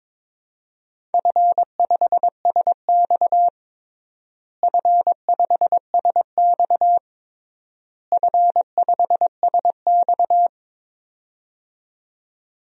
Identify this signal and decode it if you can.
Morse code, heard as a keyed tone, about 22 wpm, 700 Hz. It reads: F5SX F5SX F5SX
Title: DXCC-100 Call Signs - 4 Characters - Repeated 3x - 22wpm